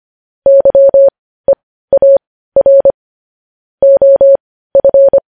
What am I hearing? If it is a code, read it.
Morse code, heard as a keyed tone, about 25 wpm, spaced as for 15 wpm, 550 Hz. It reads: YEAR OF
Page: news headlines in morse code at 15 wpm